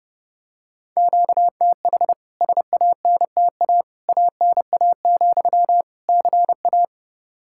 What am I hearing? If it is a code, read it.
Morse code, heard as a keyed tone, about 32 wpm, 700 Hz. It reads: QTH SANTA ANA, CA
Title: K1USN SST QSO Practice 30wpm